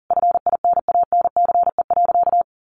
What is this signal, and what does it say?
Morse code, heard as a keyed tone, about 40 wpm, 700 Hz. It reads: FINANCE.
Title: Sentences from Top 1500 Words 40wpm